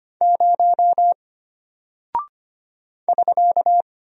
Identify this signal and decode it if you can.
Morse code, heard as a keyed tone, about 25 wpm, 700 Hz. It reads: 0 <SK>
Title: Single Character (Letter:Number:Punctuation:Prosign) Alt 25wpm